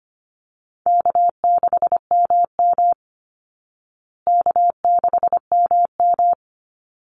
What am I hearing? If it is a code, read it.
Morse code, heard as a keyed tone, about 25 wpm, 700 Hz. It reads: K6MM K6MM
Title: Easy Call Signs Repeated 3x - 25wpm